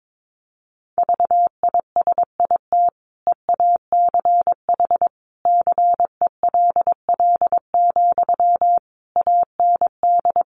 Morse code, heard as a keyed tone, about 22 wpm, 700 Hz. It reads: VISIT EACH CELL, AND
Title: Sentences from Top 1100 Words 22wpm